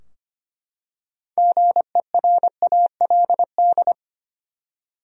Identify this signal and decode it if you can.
Morse code, heard as a keyed tone, about 25 wpm, 700 Hz. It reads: GERALD
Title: USA Common Names - 25wpm